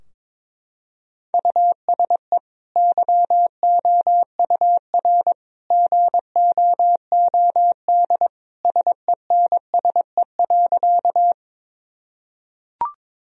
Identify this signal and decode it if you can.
Morse code, heard as a keyed tone, about 22 wpm, 700 Hz. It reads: USE YOUR GOOD SENSE.
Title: Sentences from Top 1000 Words - Review Part 4 - 22wpm